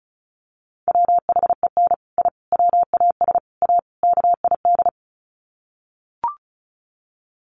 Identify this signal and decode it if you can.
Morse code, heard as a keyed tone, about 34 wpm, 700 Hz. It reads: WHEN I WAS A KID
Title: Sets of 5 Words 35wpm